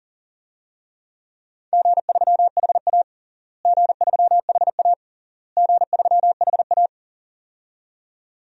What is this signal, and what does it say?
Morse code, heard as a keyed tone, about 40 wpm, 700 Hz. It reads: G3HA G3HA G3HA
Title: DXCC-100 Call Signs - 4 Characters - Repeated 3x - 40wpm